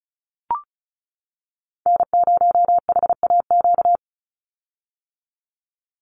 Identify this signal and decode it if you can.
Morse code, heard as a keyed tone, about 34 wpm, 700 Hz. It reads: N0HAQ
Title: USA Call Signs 35wpm